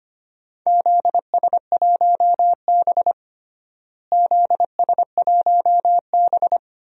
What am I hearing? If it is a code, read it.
Morse code, heard as a keyed tone, about 25 wpm, 700 Hz. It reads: ZS1B ZS1B
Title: DXCC-100 Call Signs - 4 Characters - Repeated 3x - 25wpm